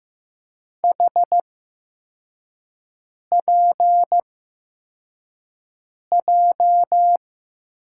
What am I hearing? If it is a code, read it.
Morse code, heard as a keyed tone, about 15 wpm, spaced as for 4 wpm, 700 Hz. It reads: HPJ
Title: USA Call Signs - 6 Characters - 8x Character Spacing - 15wpm